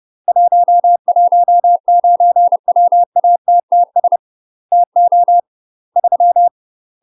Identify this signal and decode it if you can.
Morse code, heard as a keyed tone, about 30 wpm, 700 Hz. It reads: 119 WATTS TO 3
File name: UK MASTER QSOs TRAINING / 30WPM / UK MASTER QSOs 0019